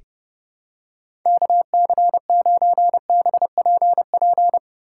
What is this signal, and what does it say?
Morse code, heard as a keyed tone, about 30 wpm, 700 Hz. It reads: KC9BPP
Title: USA Call Signs - 6 Characters - 4x Character Spacing - 30wpm